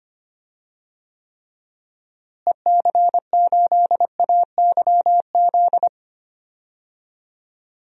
Morse code, heard as a keyed tone, about 25 wpm, 700 Hz. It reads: EC8AYZ
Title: DXCC-100 Call Signs - 6 Characters - Repeated 3x - 25wpm